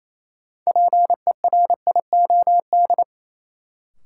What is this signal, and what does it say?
Morse code, heard as a keyed tone, about 28 wpm, 700 Hz. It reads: PERIOD
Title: Sentences from Top 800 Words - New Word Drill - 28wpm